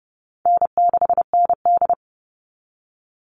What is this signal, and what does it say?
Morse code, heard as a keyed tone, about 29 wpm, 700 Hz. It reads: N6ND
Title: All Cut Numbers - 4-Digit 30wpm